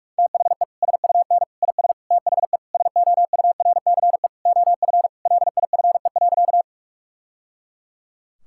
Morse code, heard as a keyed tone, about 42 wpm, 700 Hz. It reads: THE SUN IS THE SOURCE OF LIFE.
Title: Sentences from Top 1000 Words - Review Part 5 - 45wpm